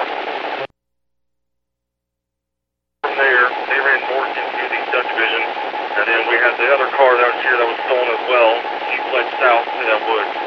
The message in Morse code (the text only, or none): none